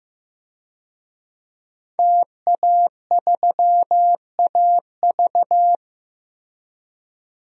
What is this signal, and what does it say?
Morse code, heard as a keyed tone, about 15 wpm, 700 Hz. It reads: TA3AV
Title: DXCC-100 Call Signs - 5 Characters - Repeated 2x - 15wpm